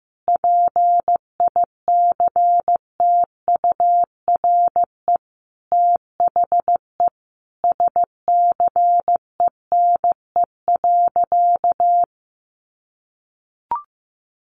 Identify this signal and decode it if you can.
Morse code, heard as a keyed tone, about 15 wpm, 700 Hz. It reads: PICTURE THE SCENE.
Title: Sentences from Top 1100 Words 15wpm